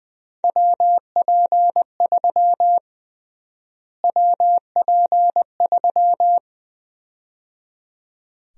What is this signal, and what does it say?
Morse code, heard as a keyed tone, about 21 wpm, 700 Hz. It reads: WP3 WP3
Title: USA Call Sign Prefixes - Repeated 2x - 20wpm